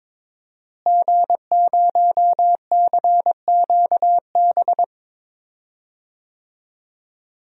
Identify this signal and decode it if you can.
Morse code, heard as a keyed tone, about 22 wpm, 700 Hz. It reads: G0CQB
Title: DXCC-100 Call Signs - 5 Characters - Repeated 2x - 22wpm